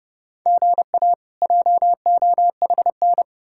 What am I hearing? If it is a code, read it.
Morse code, heard as a keyed tone, about 40 wpm, spaced as for 35 wpm, 700 Hz. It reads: GA JOHN
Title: Speed-Racing Common QSO Phrases Encore - 30wpm